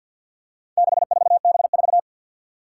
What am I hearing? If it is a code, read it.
Morse code, heard as a keyed tone, about 51 wpm, 700 Hz. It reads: 64B4
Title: Four Challenging Letter-Number Combinations 50wpm